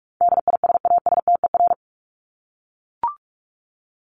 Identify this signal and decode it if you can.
Morse code, heard as a keyed tone, about 44 wpm, 700 Hz. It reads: DISASTER
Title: Top 1501-1750 Words 45wpm